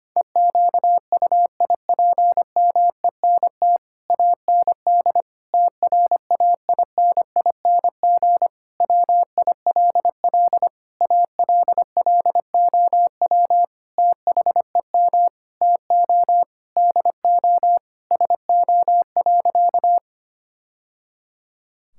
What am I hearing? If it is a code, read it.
Morse code, heard as a keyed tone, about 25 wpm, 700 Hz. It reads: EQUIPMENT AND TRAINING WILL ALLOW THEM TO DO SO.
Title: Sentences from Top 1000 Words - Review Part 5 - 25wpm